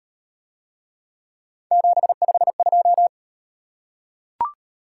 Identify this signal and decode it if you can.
Morse code, heard as a keyed tone, about 38 wpm, 700 Hz. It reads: 752